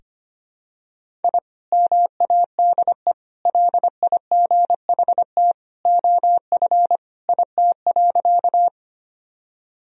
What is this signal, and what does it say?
Morse code, heard as a keyed tone, about 25 wpm, 700 Hz. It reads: I MADE LIGHT OF IT.
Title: Sentences from Top 1000 Words - Review Part 1 - 25wpm